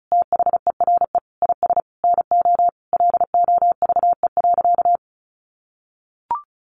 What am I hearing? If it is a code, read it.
Morse code, heard as a keyed tone, about 34 wpm, 700 Hz. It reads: THERE IS NO LOVE.
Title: Sentences from Top 1200 Words 35wpm